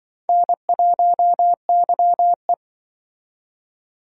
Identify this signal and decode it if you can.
Morse code, heard as a keyed tone, about 25 wpm, 700 Hz. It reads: N1YE